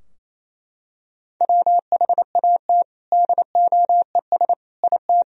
Morse code, heard as a keyed tone, about 28 wpm, 700 Hz. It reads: WHAT DOES IT
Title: Sentences from Top 1000 Words - Review Part 2 - 28wpm